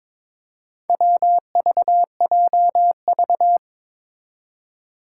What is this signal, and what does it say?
Morse code, heard as a keyed tone, about 23 wpm, 700 Hz. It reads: WVJV
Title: Four Challenging Letter-Number Combinations 22wpm